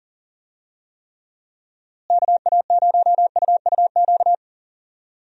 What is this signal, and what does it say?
Morse code, heard as a keyed tone, about 39 wpm, 700 Hz. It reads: KA0UUQ
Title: USA Call Signs - 6 Characters - Repeated 3x - 40wpm